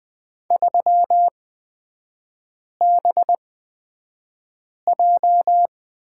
Morse code, heard as a keyed tone, about 20 wpm, spaced as for 5 wpm, 700 Hz. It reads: 3BJ
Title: 224 Lesson 36e - BT - Three Characters 20wpm